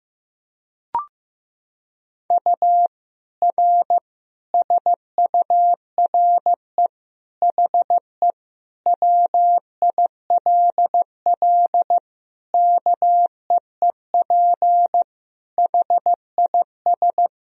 Morse code, heard as a keyed tone, about 15 wpm, 700 Hz. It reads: U R SURE HE WILL KEEP HIS
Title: Sentences from Top 300 Words and Abbreviations 15wpm